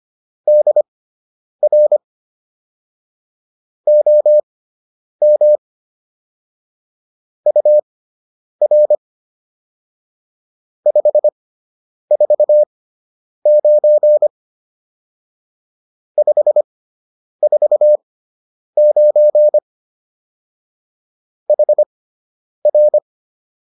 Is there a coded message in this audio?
Yes: Morse code from 0.5 to 23.0 s, heard as a keyed tone, about 25 wpm, spaced as for 9 wpm, 600 Hz. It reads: DR OM UR 549 549 HR